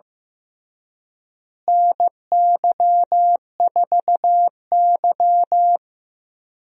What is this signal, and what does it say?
Morse code, heard as a keyed tone, about 15 wpm, 700 Hz. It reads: NY4Y
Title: USA Call Signs - 4 Characters - Repeated 3x - 15wpm